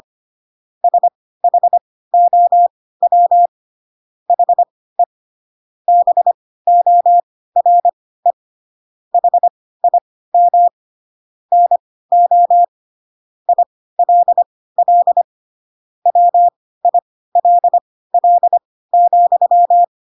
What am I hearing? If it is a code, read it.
Morse code, heard as a keyed tone, about 25 wpm, spaced as for 16 wpm, 700 Hz. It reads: SHOW HE BORE HIM NO ILL WILL,
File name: Pinocchio / Pinocchio ch2 10 25 ik1hge